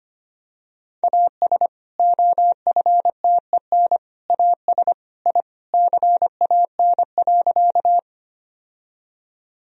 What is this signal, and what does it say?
Morse code, heard as a keyed tone, about 25 wpm, 700 Hz. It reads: AS OFTEN AS I CAN.
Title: Sentences from Top 1000 Words - Review Part 1 - 25wpm